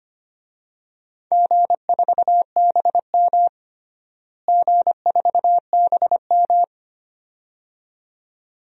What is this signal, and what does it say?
Morse code, heard as a keyed tone, about 25 wpm, 700 Hz. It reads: G4BM G4BM
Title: DXCC-100 Call Signs - 4 Characters - Repeated 2x - 25wpm